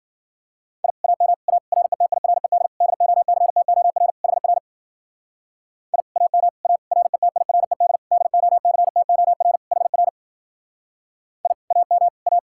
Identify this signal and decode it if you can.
Morse code, heard as a keyed tone, about 51 wpm, 700 Hz. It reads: I AM A RETIRED DOCTOR HR I AM A RETIRED DOCTOR HR I AM A